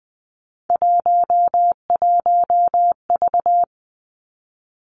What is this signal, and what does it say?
Morse code, heard as a keyed tone, about 20 wpm, 700 Hz. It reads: 11V
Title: Three Challenging Letter-Number Combinations 20wpm